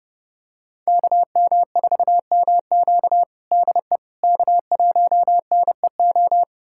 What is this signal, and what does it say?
Morse code, heard as a keyed tone, about 30 wpm, 700 Hz. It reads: KM4MQ DE K1NEO
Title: USA Callsign DE Callsign - 5 Characters - 30wpm